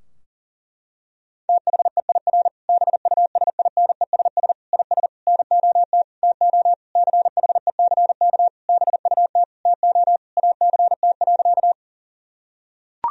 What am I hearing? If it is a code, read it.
Morse code, heard as a keyed tone, about 40 wpm, 700 Hz. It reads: THEIR BUSINESS IS NOT TO CHECK BUT TO ACT.
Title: Sentences from Top 1000 Words - Review Part 3 - 40wpm